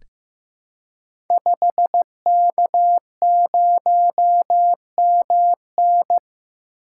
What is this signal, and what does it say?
Morse code, heard as a keyed tone, about 15 wpm, 700 Hz. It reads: 5K0MN
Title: Code Groups - 5 Letter:Numbers - 15wpm